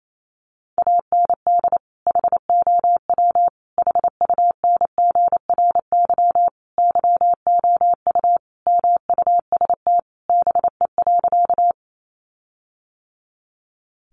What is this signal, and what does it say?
Morse code, heard as a keyed tone, about 28 wpm, 700 Hz. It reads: AND HOW HUNGRY YOU MUST BE.
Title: Sentences from Top 1400 Words 28wpm